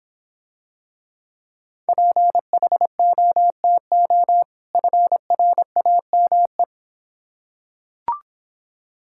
Morse code, heard as a keyed tone, about 26 wpm, 700 Hz. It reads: PHOTO FRAME